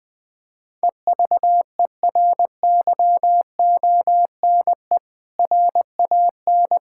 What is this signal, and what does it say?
Morse code, heard as a keyed tone, about 21 wpm, 700 Hz. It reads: EVERYONE RAN
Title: Sentences from Top 1600 Words 20wpm